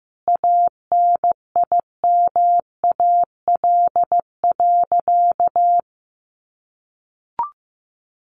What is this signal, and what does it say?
Morse code, heard as a keyed tone, about 15 wpm, 700 Hz. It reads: ANIMAL.
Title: Easy Phrases 15wpm